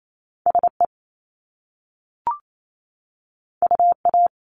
Morse code, heard as a keyed tone, about 28 wpm, 700 Hz. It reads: SE UA
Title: Top Two Letter Combinations in Words 28wpm